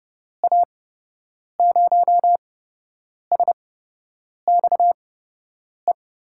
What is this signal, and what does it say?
Morse code, heard as a keyed tone, about 28 wpm, spaced as for 8 wpm, 700 Hz. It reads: A0SXE
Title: USA Call Signs - 6 Characters - 8x Character Spacing - 30wpm